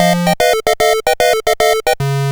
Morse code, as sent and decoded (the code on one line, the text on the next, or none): none